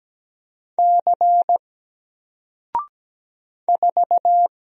Morse code, heard as a keyed tone, about 17 wpm, 700 Hz. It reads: C 4
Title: Single Letter-Number 17wpm